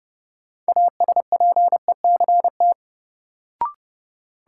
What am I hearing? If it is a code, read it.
Morse code, heard as a keyed tone, about 29 wpm, 700 Hz. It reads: ASPECT
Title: Sentences from Top 1300 Words - New Word Drill Alt - 30wpm